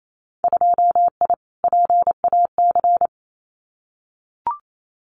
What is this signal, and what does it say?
Morse code, heard as a keyed tone, about 28 wpm, 700 Hz. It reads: 2I PAC
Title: Winter Field Day Exchanges Encore 28wpm